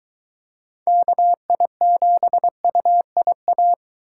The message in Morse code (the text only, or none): KI7UIA